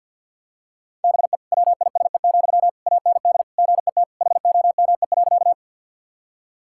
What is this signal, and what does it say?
Morse code, heard as a keyed tone, about 50 wpm, 700 Hz. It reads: BE WISE, AND GET HOME.
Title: Sentences from Top 1600 Words 50wpm